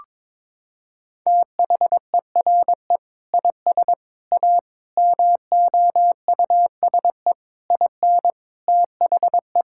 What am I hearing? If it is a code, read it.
Morse code, heard as a keyed tone, about 22 wpm, 700 Hz. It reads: THERE IS A MOUSE IN THE
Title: Sentences from Top 1000 Words - Review Part 5 - 22wpm